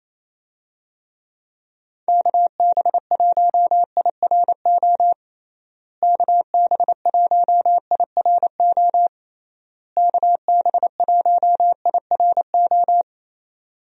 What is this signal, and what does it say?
Morse code, heard as a keyed tone, about 28 wpm, 700 Hz. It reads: KB1IRO KB1IRO KB1IRO
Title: USA Call Signs - 6 Characters - Repeated 3x - 28wpm